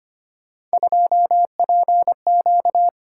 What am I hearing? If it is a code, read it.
Morse code, heard as a keyed tone, about 25 wpm, 700 Hz. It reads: 2PQ
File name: Three Letter-Number Combinations - 25wpm